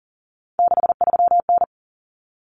Morse code, heard as a keyed tone, about 40 wpm, 700 Hz. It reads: 63N
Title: Common Cut Numbers - 3-Digit 40wpm